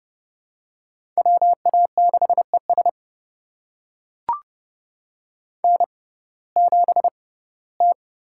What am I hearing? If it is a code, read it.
Morse code, heard as a keyed tone, about 32 wpm, spaced as for 19 wpm, 700 Hz. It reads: WA6ES N7T
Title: USA Call Signs - 6x Character Spacing - 30wpm